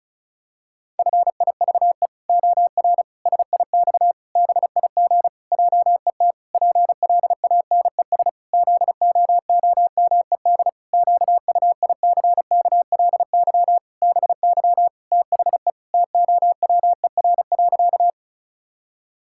Morse code, heard as a keyed tone, about 36 wpm, 700 Hz. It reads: FIVE OR SIX BIG JET PLANES ZOOMED QUICKLY BY THE TOWER.
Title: Pangram Challenge v2 - 35wpm